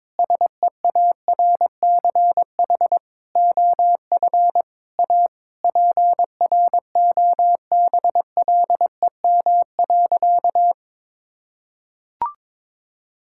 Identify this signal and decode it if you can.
Morse code, heard as a keyed tone, about 22 wpm, 700 Hz. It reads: SEARCH OF A PROBLEM.
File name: Sentences with 7-Word Phrases - 22wpm